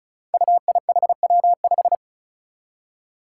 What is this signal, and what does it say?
Morse code, heard as a keyed tone, about 34 wpm, 700 Hz. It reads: UIHW5
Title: Five Challenging Letter-Number Combinations 35wpm